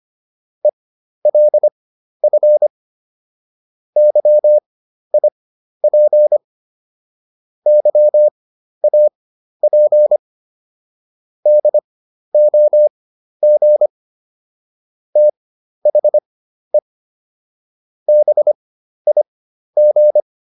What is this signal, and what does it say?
Morse code, heard as a keyed tone, about 25 wpm, spaced as for 12 wpm, 600 Hz. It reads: ELF YIP YAP DOG THE BIG